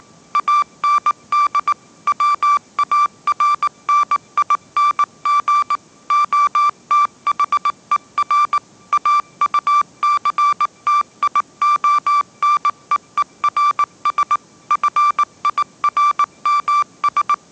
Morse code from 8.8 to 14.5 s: AUCTIONEERS